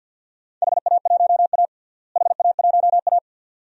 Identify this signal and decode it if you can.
Morse code, heard as a keyed tone, about 51 wpm, 700 Hz. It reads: HA1A HA1A
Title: DXCC-100 Call Signs - 4 Characters - Repeated 2x - 50wpm